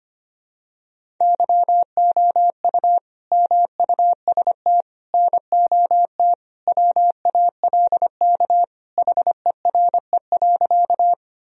Morse code, heard as a keyed tone, about 25 wpm, 700 Hz. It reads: YOU MUST NOT WALK HERE.
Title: Sentences from Top 1000 Words - Review Part 1 - 25wpm